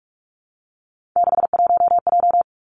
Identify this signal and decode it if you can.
Morse code, heard as a keyed tone, about 46 wpm, 700 Hz. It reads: 61J
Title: Three Challenging Letter-Number Combinations 45wpm